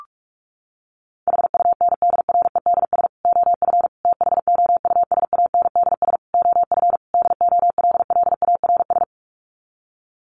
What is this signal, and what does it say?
Morse code, heard as a keyed tone, about 44 wpm, 700 Hz. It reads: HUNDREDS OF THOUSANDS OF DOLLARS
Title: Sets of 5 Words Encore 45wpm